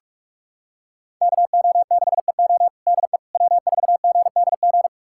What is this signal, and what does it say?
Morse code, heard as a keyed tone, about 46 wpm, 700 Hz. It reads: KO6EO DE W4GDG
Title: USA Callsign DE Callsign - 5 Characters - 45wpm